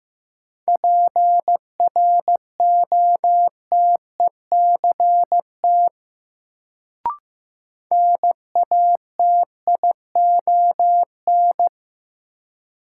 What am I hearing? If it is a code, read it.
Morse code, heard as a keyed tone, about 15 wpm, 700 Hz. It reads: PROTECT NATION
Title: Sentences from Top 900 Words - New Word Drill - 15wpm